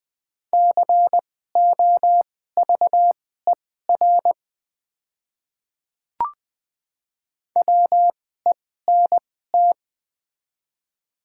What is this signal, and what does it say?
Morse code, heard as a keyed tone, about 20 wpm, spaced as for 14 wpm, 700 Hz. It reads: COVER WENT
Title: 2x Character Spacing - Top 300 Words - 20wpm